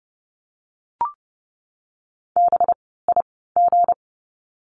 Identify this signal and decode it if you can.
Morse code, heard as a keyed tone, about 30 wpm, spaced as for 17 wpm, 700 Hz. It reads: BIG
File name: course / 30:10wpm / 237 Lesson 38f - Letter q - Words 30wpm